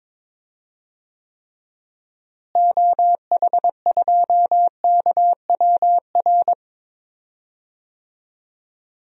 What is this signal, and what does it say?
Morse code, heard as a keyed tone, about 21 wpm, 700 Hz. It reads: OH2KWR
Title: DXCC-100 Call Signs - 6 Characters - Repeated 2x - 22wpm